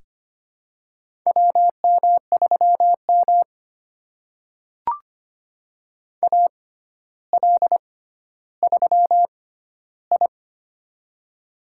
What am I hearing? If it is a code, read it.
Morse code, heard as a keyed tone, about 25 wpm, spaced as for 13 wpm, 700 Hz. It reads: WM3M AL3I
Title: USA Call Signs - 4 Characters - 6x Character Spacing - 25wpm